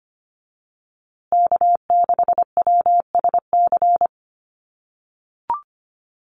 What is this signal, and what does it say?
Morse code, heard as a keyed tone, about 25 wpm, 700 Hz. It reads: K6WSC
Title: USA Call Signs Repeated 2x 25wpm